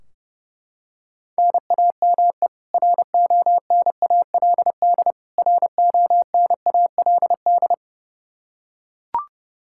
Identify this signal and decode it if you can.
Morse code, heard as a keyed tone, about 32 wpm, 700 Hz. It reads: NAME RONALD RONALD
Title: Practice QSO - Exchange 1 - Beginning - 30wpm